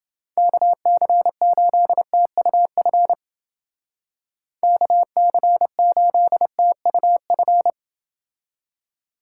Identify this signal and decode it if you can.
Morse code, heard as a keyed tone, about 29 wpm, 700 Hz. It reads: KC8TUF KC8TUF